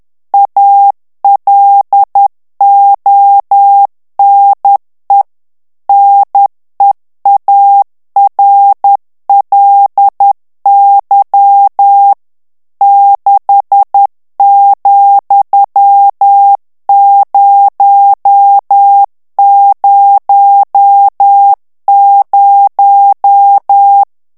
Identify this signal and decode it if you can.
Morse code, heard as a keyed tone, about 10 wpm, 800 Hz. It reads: ALONE NEARLY 6,000
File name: morse mp3 / ANGLAIS texte clair 2 vitesse 10 mots minute